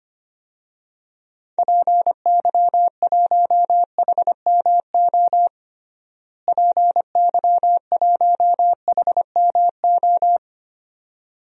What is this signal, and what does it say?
Morse code, heard as a keyed tone, about 25 wpm, 700 Hz. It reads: PY1HMO PY1HMO
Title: DXCC-100 Call Signs - 6 Characters - Repeated 2x - 25wpm